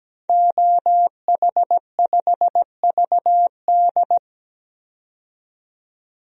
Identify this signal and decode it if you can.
Morse code, heard as a keyed tone, about 17 wpm, 700 Hz. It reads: OH5VD
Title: DXCC-100 Call Signs - 5 Characters - 17wpm